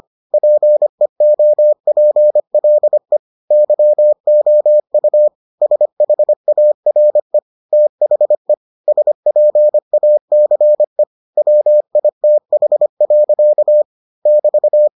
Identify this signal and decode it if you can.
Morse code, heard as a keyed tone, about 25 wpm, 600 Hz. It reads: PEOPLE YOU SHARE THE SPACE WITH. <BT>